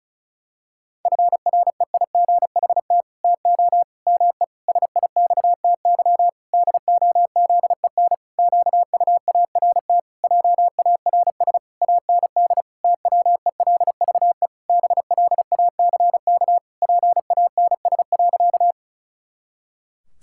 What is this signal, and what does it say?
Morse code, heard as a keyed tone, about 36 wpm, 700 Hz. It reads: FREIGHT TO ME SIXTY DOZEN QUART JARS AND TWELVE BLACK PANS.
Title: Pangram Challenge v2 - 35wpm